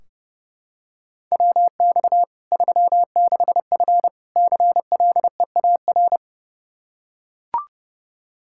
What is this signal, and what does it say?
Morse code, heard as a keyed tone, about 30 wpm, 700 Hz. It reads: WX 36F CLEAR
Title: Practice QSO - Exchange 2 - Info - 30wpm